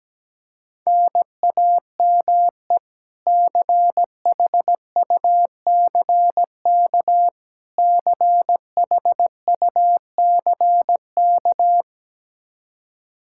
Morse code, heard as a keyed tone, about 17 wpm, 700 Hz. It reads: NAME CHUCK CHUCK
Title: K1USN SST QSO Practice 17wpm